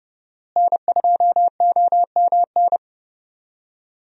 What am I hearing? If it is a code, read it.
Morse code, heard as a keyed tone, about 30 wpm, 700 Hz. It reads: N2OMN
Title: USA Call Signs - 5 Characters - Repeated 3x - 30wpm